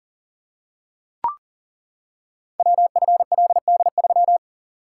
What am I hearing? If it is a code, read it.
Morse code, heard as a keyed tone, about 42 wpm, 690 Hz. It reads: WFLD3
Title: Code Groups - 5 Letter:Numbers - 40wpm